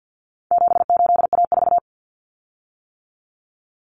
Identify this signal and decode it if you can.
Morse code, heard as a keyed tone, about 50 wpm, 700 Hz. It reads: 78A4
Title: Common Cut Numbers - 4-Digit 50wpm